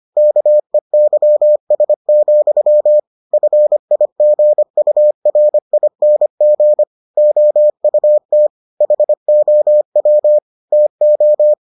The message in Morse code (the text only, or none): KEYS, FIGURING OUT HOW TO